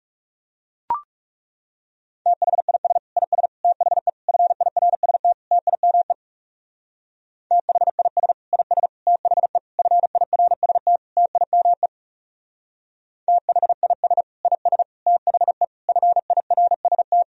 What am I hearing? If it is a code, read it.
Morse code, heard as a keyed tone, about 37 wpm, 700 Hz. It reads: THIS IS THE FIRST TIME THIS IS THE FIRST TIME THIS IS THE FIRST
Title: Speed-Racing - Sets of 5 Words 30wpm